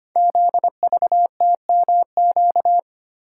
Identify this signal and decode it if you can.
Morse code, heard as a keyed tone, about 26 wpm, 700 Hz. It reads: ZVTMQ
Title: Code Groups - 5 Letters - 25wpm